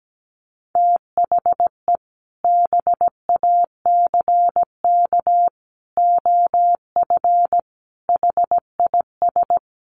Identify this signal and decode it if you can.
Morse code, heard as a keyed tone, about 17 wpm, 700 Hz. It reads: THE BACK OF HIS
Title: Sets of 5 Words Encore 17wpm